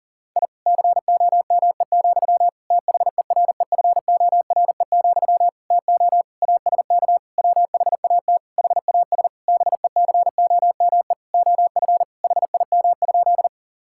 Morse code, heard as a keyed tone, about 40 wpm, 700 Hz. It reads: I COME, THEREFORE, TO ASK WHAT HAS BECOME OF HIM?
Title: Sentences from Top 1000 Words - Review Part 5 - 40wpm